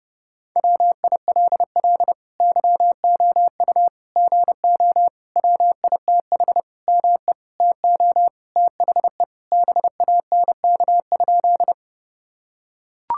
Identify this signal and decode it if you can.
Morse code, heard as a keyed tone, about 30 wpm, 700 Hz. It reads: WILL YOU GO WITH ME TO THE BANK?
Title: Sentences from Top 1000 Words - Review Part 6 - 30wpm